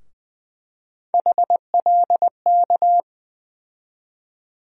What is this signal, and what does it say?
Morse code, heard as a keyed tone, about 21 wpm, 700 Hz. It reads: HLK
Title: USA Call Sign Postfixes - 3 Letters 20wpm